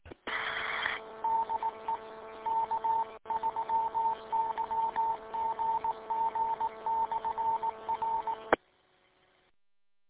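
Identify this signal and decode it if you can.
Morse code, heard as a keyed tone, about 18 wpm, 900 Hz, with noise. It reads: DE K3YGG/R
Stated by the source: Recorded from the W8UM 145.230 audio archive.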